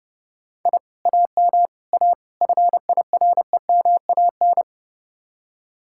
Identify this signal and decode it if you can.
Morse code, heard as a keyed tone, about 32 wpm, 700 Hz. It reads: I AM A FIREMAN